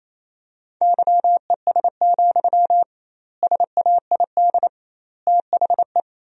Morse code, heard as a keyed tone, about 28 wpm, 700 Hz. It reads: YES, SAID THE
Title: 2x Word-Spacing - Sentences from Top 500 Words - 28wpm